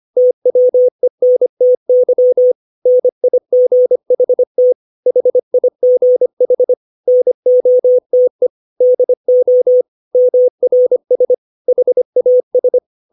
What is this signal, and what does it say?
Morse code, heard as a keyed tone, about 25 wpm, 500 Hz. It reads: TWENTY NIGHT HIGH NOTE DO MRS HAS
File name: 632 words / 632 Words 25WPM